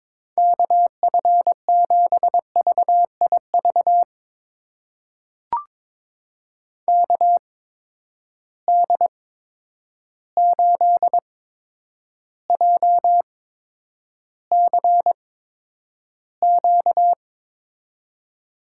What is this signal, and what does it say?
Morse code, heard as a keyed tone, about 21 wpm, spaced as for 9 wpm, 700 Hz. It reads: KF7VIV KD8JCQ
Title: USA Call Signs - 6 Characters - 8x Character Spacing - 22wpm